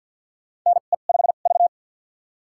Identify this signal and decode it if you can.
Morse code, heard as a keyed tone, about 50 wpm, spaced as for 36 wpm, 700 Hz. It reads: NE5V
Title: USA Call Signs - 4 Characters - 2x Character Spacing - 50wpm